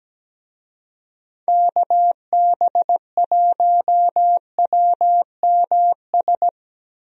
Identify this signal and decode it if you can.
Morse code, heard as a keyed tone, about 17 wpm, 700 Hz. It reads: KB1WMS
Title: USA Call Signs - 6 Characters - Repeated 2x - 17wpm